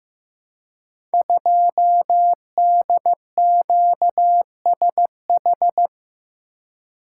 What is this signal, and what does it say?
Morse code, heard as a keyed tone, about 15 wpm, 700 Hz. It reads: 2DQSH
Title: Code Groups - 5 Letter:Numbers - 15wpm